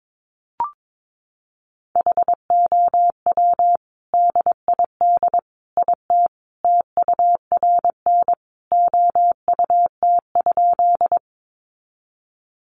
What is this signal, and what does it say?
Morse code, heard as a keyed tone, about 22 wpm, 700 Hz. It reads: HOW DID IT TURN OUT?
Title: A Sound Of Thunder 22wpm